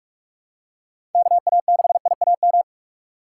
Morse code, heard as a keyed tone, about 44 wpm, 700 Hz. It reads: KA6IAM
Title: USA Call Signs - 6 Characters - 4x Character Spacing - 45wpm